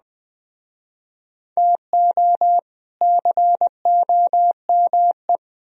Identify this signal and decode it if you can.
Morse code, heard as a keyed tone, about 20 wpm, 700 Hz. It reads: TO COME